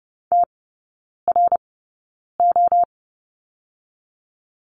Morse code, heard as a keyed tone, about 30 wpm, spaced as for 9 wpm, 700 Hz. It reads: TRO
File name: 081 Lesson 14e - Number 2 - Three Characters 30wpm